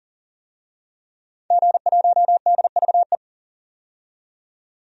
Morse code, heard as a keyed tone, about 40 wpm, 700 Hz. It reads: G1DVE
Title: DXCC-100 Call Signs - 5 Characters - 40wpm